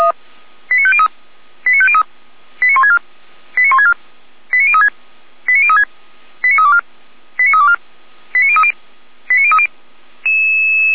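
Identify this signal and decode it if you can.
Five-tone selective call (ZVEI), heard as quick runs of five tones, tones of about 70 ms. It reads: 89503 89503 89164 89164 89036 89036 89324 89324 89030 89030
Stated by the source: ZVEI five-tone call 89050 received on SDR.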